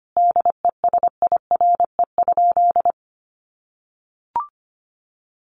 Morse code, heard as a keyed tone, about 25 wpm, 700 Hz. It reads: DESIRE?
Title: Sentences from Top 1100 Words 25wpm